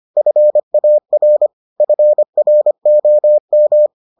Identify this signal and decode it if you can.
Morse code, heard as a keyed tone, about 25 wpm, 600 Hz. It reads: FAR FROM